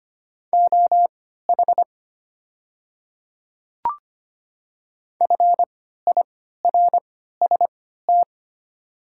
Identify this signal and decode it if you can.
Morse code, heard as a keyed tone, about 25 wpm, spaced as for 14 wpm, 690 Hz. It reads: OH FIRST
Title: 3x Character Spacing - Top 500 Words - 25wpm